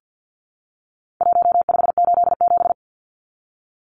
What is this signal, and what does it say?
Morse code, heard as a keyed tone, about 48 wpm, 700 Hz. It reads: 1587
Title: Numbers - 4-Digit 50wpm